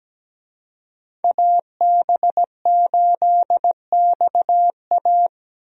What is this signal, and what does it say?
Morse code, heard as a keyed tone, about 17 wpm, 700 Hz. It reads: AB8XA
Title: USA Call Signs - 5 Characters - 17wpm